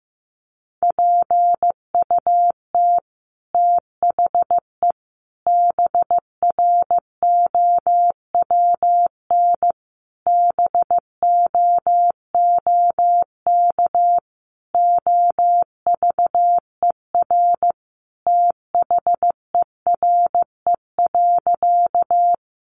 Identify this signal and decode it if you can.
Morse code, heard as a keyed tone, about 15 wpm, 700 Hz. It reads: PUT THE BROWN BOOK OVER THERE.
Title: Sentences from Top 1200 Words 15wpm